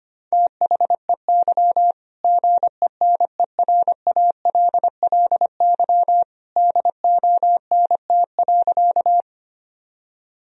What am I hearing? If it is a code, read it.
Morse code, heard as a keyed tone, about 25 wpm, 700 Hz. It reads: THEY GENERALLY DONT.
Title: Sentences from Top 1000 Words - Review Part 4 - 25wpm